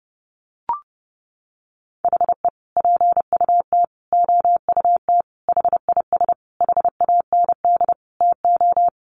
Morse code, heard as a keyed tone, about 30 wpm, 700 Hz. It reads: HE PUT OUT HIS HAND TO
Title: Sentences from Top 800 Words 30wpm